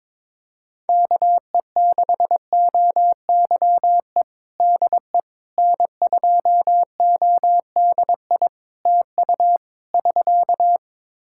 Morse code, heard as a keyed tone, about 22 wpm, 700 Hz. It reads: KE6OYE DE N2ODI TU <SK>
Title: K1USN SST QSO Practice 22wpm